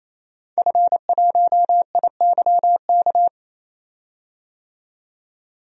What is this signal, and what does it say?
Morse code, heard as a keyed tone, about 28 wpm, 700 Hz. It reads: F1IYK
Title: DXCC-100 Call Signs - 5 Characters - 28wpm